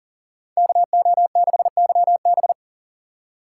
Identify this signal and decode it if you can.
Morse code, heard as a keyed tone, about 52 wpm, 700 Hz. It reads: KO6YB